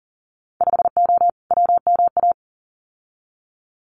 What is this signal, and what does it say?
Morse code, heard as a keyed tone, about 38 wpm, 700 Hz. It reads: 5O WMA
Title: Winter Field Day Exchanges 40wpm